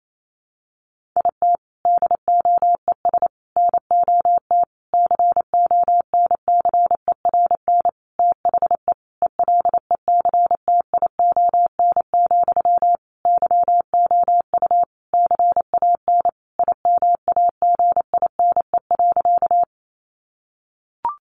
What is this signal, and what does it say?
Morse code, heard as a keyed tone, about 28 wpm, 700 Hz. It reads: IT DOES NOT CONCERN THE ELECTION, YOU CAN IMAGINE.
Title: Sentences from Top 1500 Words 28wpm